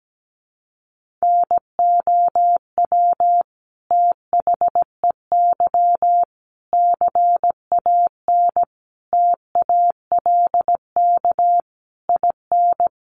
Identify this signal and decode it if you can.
Morse code, heard as a keyed tone, about 17 wpm, 700 Hz. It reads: NOW THEY CAN TALK IN
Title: Sentences from Top 700 Words 17wpm